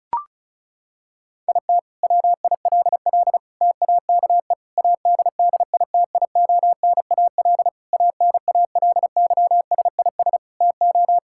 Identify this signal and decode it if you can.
Morse code, heard as a keyed tone, about 36 wpm, 700 Hz. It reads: IT WILL TAKE ADDITIONAL ANALYSIS TO
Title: Sentences from Top 1000 Words - Review Part 5 - 35wpm